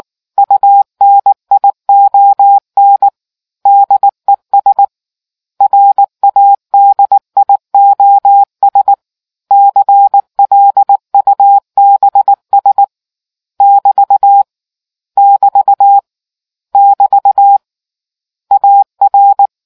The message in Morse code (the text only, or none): UNION DES RADIOS CLUBS <BT> <BT> <BT> AR